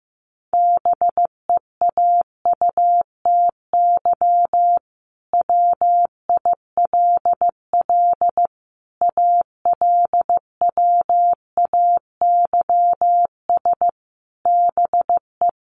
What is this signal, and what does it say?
Morse code, heard as a keyed tone, about 15 wpm, 700 Hz. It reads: BEAUTY WILL ALWAYS BE
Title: Sentences from Top 1300 Words 15wpm